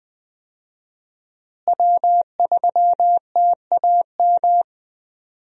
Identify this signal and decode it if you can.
Morse code, heard as a keyed tone, about 21 wpm, 700 Hz. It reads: W3TAM
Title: USA Call Signs - 5 Characters - Repeated 3x - 20wpm